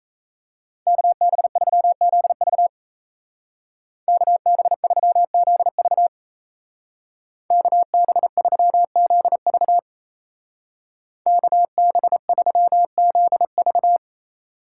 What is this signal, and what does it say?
Morse code, heard as a keyed tone, about 34 wpm, 700 Hz. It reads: KB3ZV KB3ZV KB3ZV KB3ZV
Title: Speed Racing - USA 2x2 Call Signs 28wpm